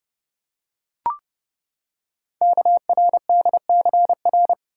Morse code, heard as a keyed tone, about 30 wpm, 700 Hz. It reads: KRDCR
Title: Code Groups - 5 Letters - 30wpm